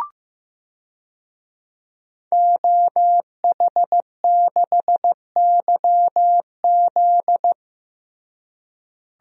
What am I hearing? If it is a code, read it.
Morse code, heard as a keyed tone, about 14 wpm, 700 Hz. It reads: OH6YZ
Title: DXCC-100 Call Signs - 5 Characters - Repeated 2x - 15wpm